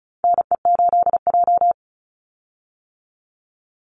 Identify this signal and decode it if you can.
Morse code, heard as a keyed tone, about 34 wpm, 700 Hz. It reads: NE8J
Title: Call Signs - 35wpm